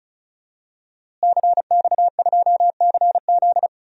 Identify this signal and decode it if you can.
Morse code, heard as a keyed tone, about 34 wpm, 700 Hz. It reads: CX2CZ
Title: DXCC-100 Call Signs - 5 Characters - Repeated 3x - 35wpm